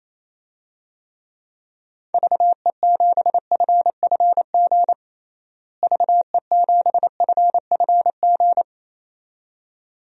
Morse code, heard as a keyed tone, about 27 wpm, 700 Hz. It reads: VE7FFG VE7FFG
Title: DXCC-100 Call Signs - 6 Characters - Repeated 2x - 28wpm